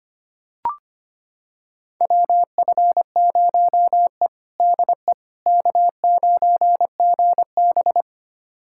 Morse code, heard as a keyed tone, about 25 wpm, 700 Hz. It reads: WF0E DE K9GB
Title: USA Callsign DE Callsign - 4 Characters - 25wpm